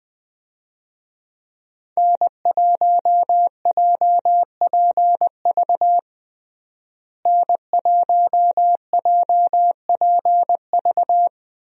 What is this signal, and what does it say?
Morse code, heard as a keyed tone, about 20 wpm, 700 Hz. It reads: N1JPV N1JPV
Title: USA Call Signs - 5 Characters - Repeated 3x - 20wpm